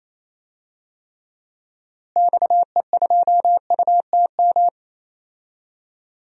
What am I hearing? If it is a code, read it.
Morse code, heard as a keyed tone, about 28 wpm, 700 Hz. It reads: XE2UTM
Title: DXCC-100 Call Signs - 6 Characters - Repeated 2x - 28wpm